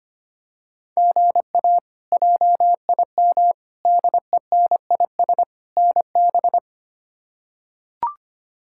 Morse code, heard as a keyed tone, about 25 wpm, 700 Hz. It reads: GA JIM DENIS NB
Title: K1USN SST QSO Practice - 25wpm